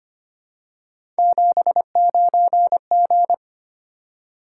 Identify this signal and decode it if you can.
Morse code, heard as a keyed tone, about 25 wpm, 700 Hz. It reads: 79G
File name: Three Letter-Number Combinations - 25wpm